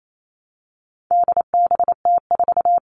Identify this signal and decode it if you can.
Morse code, heard as a keyed tone, about 28 wpm, 700 Hz. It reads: DBT4
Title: All Cut Numbers - 4-Digit 28wpm